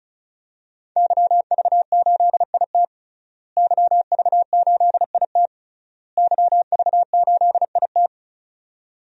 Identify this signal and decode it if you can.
Morse code, heard as a keyed tone, about 34 wpm, 700 Hz. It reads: YV8IT YV8IT YV8IT
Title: DXCC-100 Call Signs - 5 Characters - Repeated 3x - 35wpm